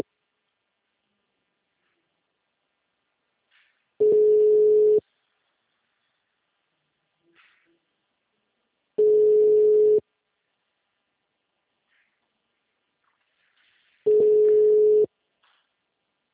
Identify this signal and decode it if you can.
Ringback tone (Europe, 425 Hz).